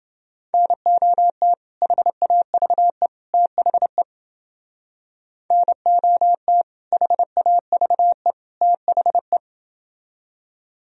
Morse code, heard as a keyed tone, about 29 wpm, 700 Hz. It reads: NOT HAVE THE NOT HAVE THE